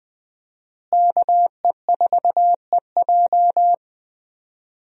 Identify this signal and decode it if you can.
Morse code, heard as a keyed tone, about 20 wpm, 700 Hz. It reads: KE4EJ
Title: USA Call Signs - 5 Characters - Repeated 3x - 20wpm